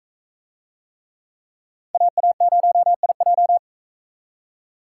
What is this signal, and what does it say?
Morse code, heard as a keyed tone, about 42 wpm, 700 Hz. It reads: AA0IJ